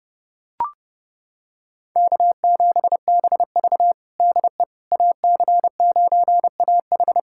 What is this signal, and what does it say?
Morse code, heard as a keyed tone, about 32 wpm, 700 Hz. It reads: K7BV DE AC9AH
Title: Practice QSO - Exchange 4 - Ending - 30wpm